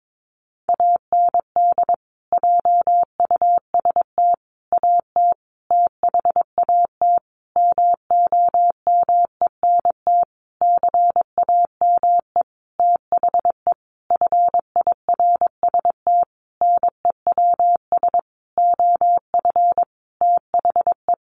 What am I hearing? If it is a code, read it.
Morse code, heard as a keyed tone, about 22 wpm, 700 Hz. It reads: AND JUST AT THAT MOMENT CAME THE FIRST NEWS OF THE
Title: Sentences from Top 1500 Words 22wpm